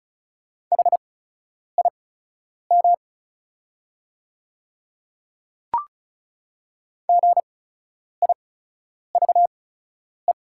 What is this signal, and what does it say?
Morse code, heard as a keyed tone, about 36 wpm, spaced as for 10 wpm, 690 Hz. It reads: HIM GIVE
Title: 8x Character Spacing - Top 300 Words - 35wpm